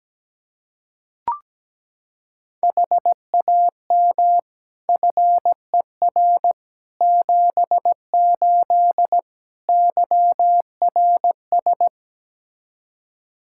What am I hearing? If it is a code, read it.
Morse code, heard as a keyed tone, about 17 wpm, 700 Hz. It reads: HAM FER 78 YRS
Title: Practice QSO - Exchange 3 - More Info - 17wpm